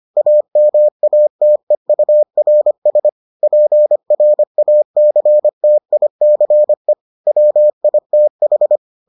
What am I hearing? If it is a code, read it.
Morse code, heard as a keyed tone, about 25 wpm, 600 Hz. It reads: AMATEURS PRACTICE WITH